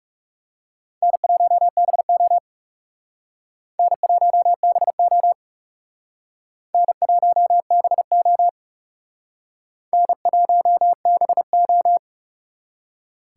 Morse code, heard as a keyed tone, about 37 wpm, 700 Hz. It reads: N1BO N1BO N1BO N1BO
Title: Speed Racing - USA 1x2 Call Signs 30wpm